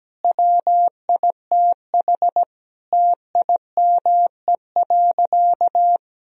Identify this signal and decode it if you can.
Morse code, heard as a keyed tone, about 17 wpm, 700 Hz. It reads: WITH TIME.
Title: Sentences from Top 1000 Words - Review Part 3 - 17wpm